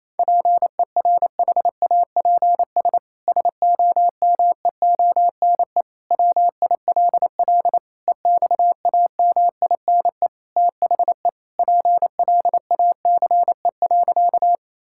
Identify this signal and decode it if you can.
Morse code, heard as a keyed tone, about 28 wpm, 700 Hz. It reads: PERHAPS SOMEONE WILL EXAMINE THE PLACE.
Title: Sentences from Top 1600 Words 28wpm